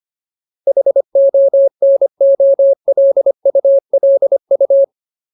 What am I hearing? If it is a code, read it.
Morse code, heard as a keyed tone, about 25 wpm, 550 Hz. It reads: HONOLULU
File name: Ultimate Word List / 25WPM Ultimate